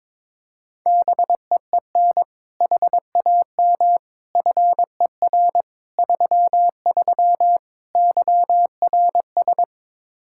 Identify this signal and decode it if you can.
Morse code, heard as a keyed tone, about 21 wpm, 700 Hz. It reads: BEEN HAM FER 33 YRS
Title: Practice QSO - Exchange 3 - More Info - 22wpm